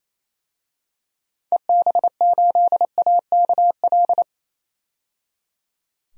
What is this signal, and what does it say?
Morse code, heard as a keyed tone, about 27 wpm, 700 Hz. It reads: EB8AKL
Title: DXCC-100 Call Signs - 6 Characters - 28wpm